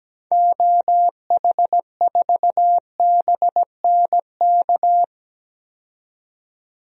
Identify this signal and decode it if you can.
Morse code, heard as a keyed tone, about 17 wpm, 700 Hz. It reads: OH4BNK
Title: DXCC-100 Call Signs - 6 Characters - 17wpm